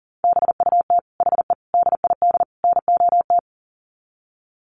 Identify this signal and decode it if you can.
Morse code, heard as a keyed tone, about 40 wpm, 700 Hz. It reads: BUT HE DID NOT
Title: Sets of 4 Words 40wpm